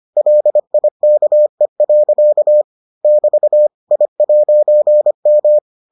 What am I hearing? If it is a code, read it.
Morse code, heard as a keyed tone, about 25 wpm, 600 Hz. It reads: LIKE. <BT> I'M